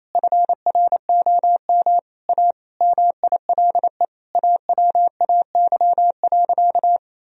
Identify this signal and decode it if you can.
Morse code, heard as a keyed tone, about 28 wpm, 700 Hz. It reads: FROM A MILE AWAY.
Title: Sentences from Top 1000 Words - Review Part 3 - 28wpm